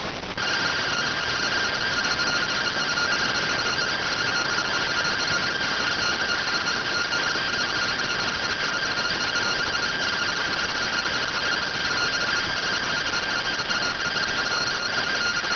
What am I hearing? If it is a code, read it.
Two-tone teleprinter signal, about 170 Hz shift.